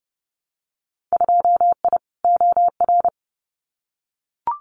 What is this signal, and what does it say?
Morse code, heard as a keyed tone, about 30 wpm, 700 Hz. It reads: 2I OR
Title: Winter Field Day Exchanges Encore 30wpm